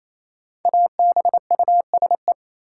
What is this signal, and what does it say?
Morse code, heard as a keyed tone, about 27 wpm, 700 Hz. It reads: ABUSE
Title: Sentences from Top 1300 Words - New Word Drill - 28wpm